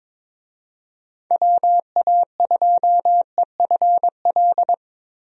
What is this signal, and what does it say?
Morse code, heard as a keyed tone, about 22 wpm, 700 Hz. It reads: WA2EFL
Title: USA Call Signs - 6 Characters - Repeated 3x - 22wpm